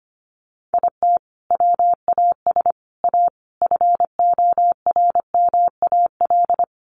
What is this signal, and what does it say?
Morse code, heard as a keyed tone, about 25 wpm, 700 Hz. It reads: IT WAS A FORMAL
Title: Sentences from Top 1300 Words 25wpm